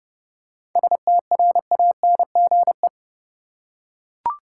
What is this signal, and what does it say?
Morse code, heard as a keyed tone, about 30 wpm, 700 Hz. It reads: STRANGE
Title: Seven Letter Words - 30wpm